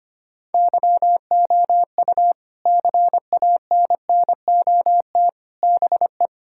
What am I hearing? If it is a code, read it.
Morse code, heard as a keyed tone, about 25 wpm, 700 Hz. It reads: YOU CANNOT BE
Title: Sentences from Top 1000 Words - Review Part 6 - 25wpm